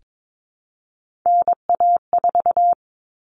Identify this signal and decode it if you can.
Morse code, heard as a keyed tone, about 22 wpm, 700 Hz. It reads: NA4
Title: All Cut Numbers - 3-Digit 22wpm